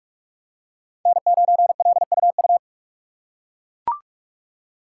Morse code, heard as a keyed tone, about 44 wpm, 690 Hz. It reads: N9LUU
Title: USA Call Signs - 5 Characters - 6x Character Spacing - 45wpm